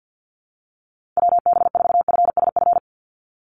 Morse code, heard as a keyed tone, about 50 wpm, 700 Hz. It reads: WB4FSF
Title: Call Signs - 50wpm